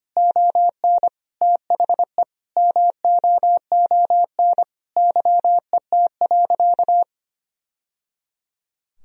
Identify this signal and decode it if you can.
Morse code, heard as a keyed tone, about 25 wpm, 700 Hz. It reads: ON THE MOON YET.
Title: Sentences from Top 1000 Words - Review Part 3 - 25wpm